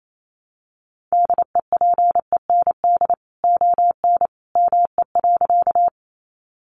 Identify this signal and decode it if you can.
Morse code, heard as a keyed tone, about 28 wpm, 700 Hz. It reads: DEPEND ON ME.
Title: Sentences from Top 1300 Words 28wpm